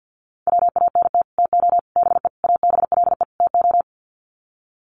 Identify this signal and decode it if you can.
Morse code, heard as a keyed tone, about 50 wpm, 700 Hz. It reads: WANT TO BE ABLE TO
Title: Sets of 5 Words 50wpm